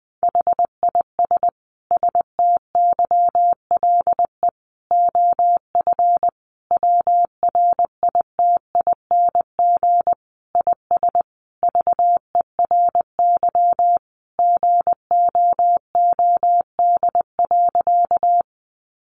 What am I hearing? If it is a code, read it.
Morse code, heard as a keyed tone, about 20 wpm, 700 Hz. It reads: HIS STYLE OF WRITING IS VERY GOOD.
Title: Sentences from Top 800 Words 20wpm